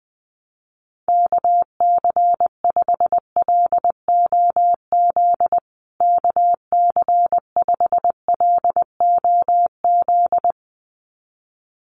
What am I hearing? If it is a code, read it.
Morse code, heard as a keyed tone, about 20 wpm, 700 Hz. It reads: KC5LOZ KC5LOZ
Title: USA Call Signs Repeated 2x 20wpm